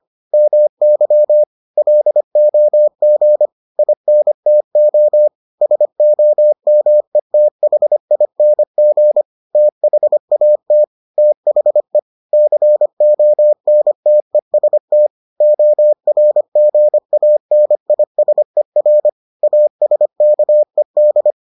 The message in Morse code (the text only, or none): MY LOG INTO SOMETHING THAT THE CONTEST ORGANISER ASKED